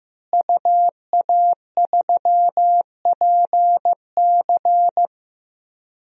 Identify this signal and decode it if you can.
Morse code, heard as a keyed tone, about 15 wpm, 700 Hz. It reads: UA3PC
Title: DXCC-100 Call Signs - 5 Characters - Repeated 3x - 15wpm